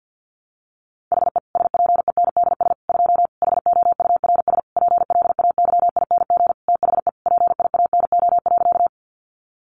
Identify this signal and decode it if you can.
Morse code, heard as a keyed tone, about 50 wpm, 700 Hz. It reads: HE SPENDS 2 HOURS PLAYING THE PIANO.
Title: Additional Easy Phrases 50wpm